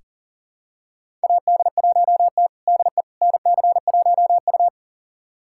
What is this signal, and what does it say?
Morse code, heard as a keyed tone, about 40 wpm, 700 Hz. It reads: AD1T DE NC1U
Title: USA Callsign DE Callsign - 4 Characters - 40wpm